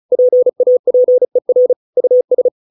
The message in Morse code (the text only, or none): PAPER US